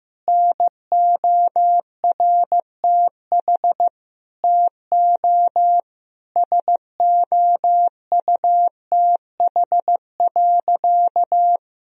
Morse code, heard as a keyed tone, about 15 wpm, 700 Hz. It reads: NORTH TO SOUTH.
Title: Sentences from Top 1000 Words - Review Part 2 - 15wpm